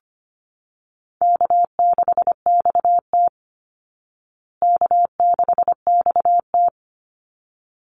Morse code, heard as a keyed tone, about 25 wpm, 700 Hz. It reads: K6XT K6XT
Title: Easy Call Signs Repeated 2x - 25wpm